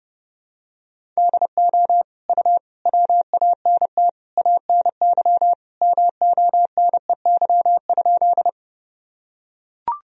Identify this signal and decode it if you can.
Morse code, heard as a keyed tone, about 32 wpm, 700 Hz. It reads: DO U WANT ANY MONEY?
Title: Sentences from Top 400 Words and Abbreviations 30wpm